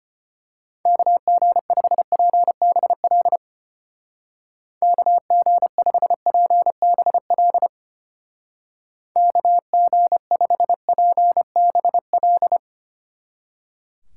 Morse code, heard as a keyed tone, about 30 wpm, 700 Hz. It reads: KG5PBL KG5PBL KG5PBL
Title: Speed Racing - USA 2x3 Call Signs 25wpm